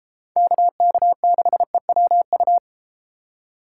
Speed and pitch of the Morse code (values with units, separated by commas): 32 wpm, 700 Hz